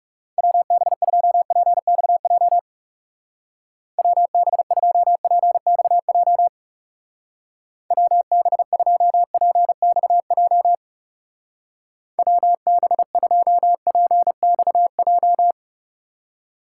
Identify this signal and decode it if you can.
Morse code, heard as a keyed tone, about 38 wpm, 700 Hz. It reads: WB2PXJ WB2PXJ WB2PXJ WB2PXJ